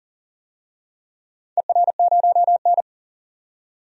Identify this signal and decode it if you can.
Morse code, heard as a keyed tone, about 40 wpm, 700 Hz. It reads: ER0N
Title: DXCC-100 Call Signs - 4 Characters - 40wpm